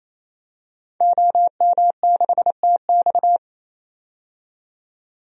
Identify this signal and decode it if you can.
Morse code, heard as a keyed tone, about 28 wpm, 700 Hz. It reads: OM6TX
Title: DXCC-100 Call Signs - 5 Characters - 28wpm